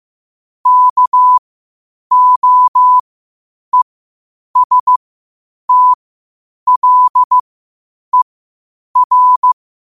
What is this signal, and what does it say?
Morse code, heard as a keyed tone, about 15 wpm, spaced as for 8 wpm, 1000 Hz. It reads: KOESTLER
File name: qotdfiles / 8wpm